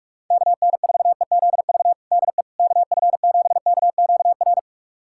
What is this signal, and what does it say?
Morse code, heard as a keyed tone, about 41 wpm, 700 Hz. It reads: KN4EZV DE KF7KQR
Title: USA Callsign DE Callsign 45wpm